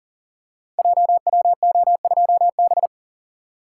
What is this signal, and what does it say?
Morse code, heard as a keyed tone, about 42 wpm, 700 Hz. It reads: JWO2B
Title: Code Groups - 5 Letter:Numbers - 40wpm